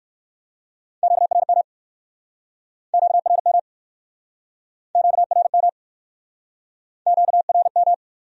Morse code, heard as a keyed tone, about 60 wpm, 700 Hz. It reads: QRM QRM QRM QRM